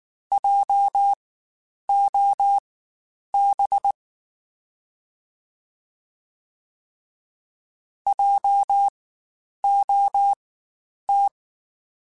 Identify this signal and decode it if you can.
Morse code, heard as a keyed tone, about 19 wpm, spaced as for 9 wpm, 800 Hz. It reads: JOB JOT
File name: Spaced Alphas / Spaced Alphas mp3s / Js sp @18